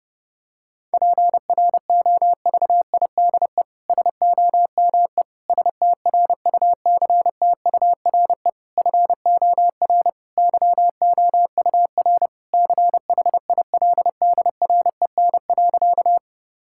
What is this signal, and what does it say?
Morse code, heard as a keyed tone, about 30 wpm, 700 Hz. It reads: PROVIDE SOME STRUCTURE FOR YOUR CHILDREN.
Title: Sentences from Top 1000 Words - Review Part 6 - 30wpm